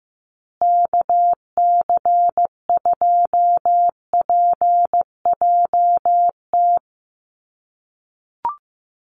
Morse code, heard as a keyed tone, about 15 wpm, 700 Hz. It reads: KC2PJT
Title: USA Call Signs Repeated 2x 15wpm